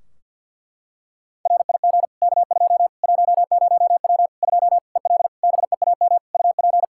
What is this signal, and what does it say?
Morse code, heard as a keyed tone, about 46 wpm, 700 Hz. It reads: RIG K2 10W 2 EL BEAM UP
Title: Practice QSO - Exchange 2 - Info - 50wpm